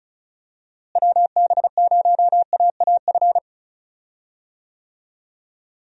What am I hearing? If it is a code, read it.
Morse code, heard as a keyed tone, about 36 wpm, 700 Hz. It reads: WB0AAF
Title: USA Call Signs - 6 Characters - Repeated 3x - 35wpm